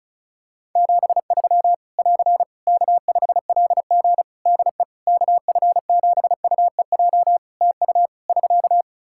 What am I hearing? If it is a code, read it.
Morse code, heard as a keyed tone, about 36 wpm, 700 Hz. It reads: 73 <AR> K5LG DE KF7UEJ TU <SK>
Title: K1USN SST QSO Practice 35wpm